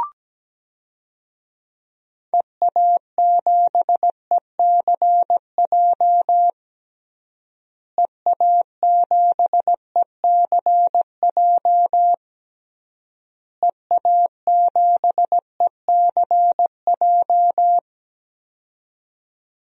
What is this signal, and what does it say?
Morse code, heard as a keyed tone, about 17 wpm, 700 Hz. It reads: EA7ECJ EA7ECJ EA7ECJ
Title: DXCC-100 Call Signs - 6 Characters - Repeated 3x - 17wpm